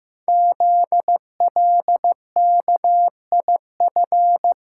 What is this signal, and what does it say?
Morse code, heard as a keyed tone, about 15 wpm, 700 Hz. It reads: ZLKIF
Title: Code Groups - 5 Letters - 15wpm